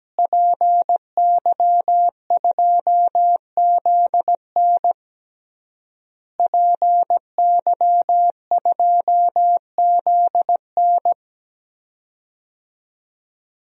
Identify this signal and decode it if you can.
Morse code, heard as a keyed tone, about 17 wpm, 700 Hz. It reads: PY2ZN PY2ZN
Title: DXCC-100 Call Signs - 5 Characters - Repeated 3x - 17wpm